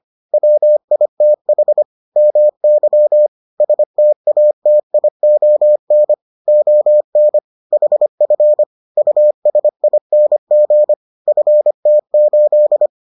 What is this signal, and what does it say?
Morse code, heard as a keyed tone, about 25 wpm, 600 Hz. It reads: WITH MY STATION ON HF USING FT8